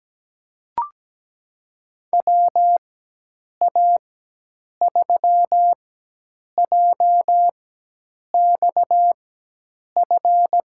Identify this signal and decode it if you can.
Morse code, heard as a keyed tone, about 17 wpm, spaced as for 8 wpm, 700 Hz. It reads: WA3JXF
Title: USA Call Signs - 6 Characters - 4x Character Spacing - 17wpm